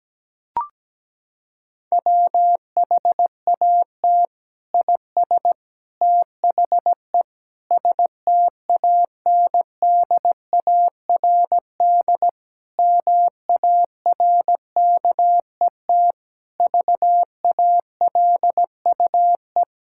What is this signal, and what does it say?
Morse code, heard as a keyed tone, about 17 wpm, 700 Hz. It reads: WHAT IS THE STANDARD MARKET VALUE
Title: Sentences from Top 1000 Words - Review Part 6 - 17wpm